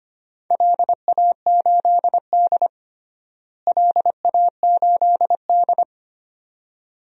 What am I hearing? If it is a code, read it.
Morse code, heard as a keyed tone, about 25 wpm, 700 Hz. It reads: LA8D LA8D
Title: DXCC-100 Call Signs - 4 Characters - Repeated 2x - 25wpm